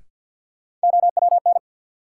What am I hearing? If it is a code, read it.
Morse code, heard as a keyed tone, about 53 wpm, 700 Hz. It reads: OWN
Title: Sets of 2 Words - New Word Drill - 50wpm